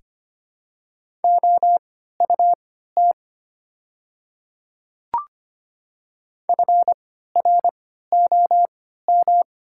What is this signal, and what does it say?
Morse code, heard as a keyed tone, about 25 wpm, spaced as for 14 wpm, 700 Hz. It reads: OUT FROM
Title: 3x Character Spacing - Top 100 Words - 25wpm